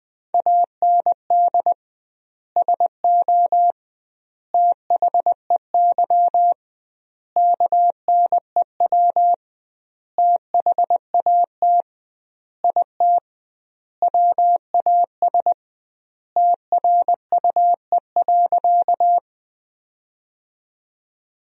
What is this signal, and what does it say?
Morse code, heard as a keyed tone, about 20 wpm, 700 Hz. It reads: AND SO THEY KNEW THAT IT WAS TRUE.
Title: 2x Word-Spacing - Sentences from Top 400 Words - 20wpm